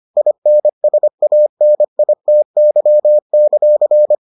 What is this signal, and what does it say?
Morse code, heard as a keyed tone, about 25 wpm, 600 Hz. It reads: INSANITY;